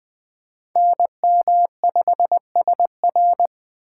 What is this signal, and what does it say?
Morse code, heard as a keyed tone, about 23 wpm, 700 Hz. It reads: NM5SR
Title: Speed Racing - USA Call Signs 20wpm